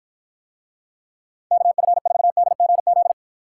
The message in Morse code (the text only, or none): KF4DGZ